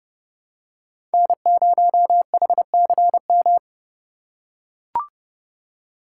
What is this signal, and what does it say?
Morse code, heard as a keyed tone, about 30 wpm, 700 Hz. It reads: N0HCM
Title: USA Call Signs - 5 Characters - 6x Character Spacing - 30wpm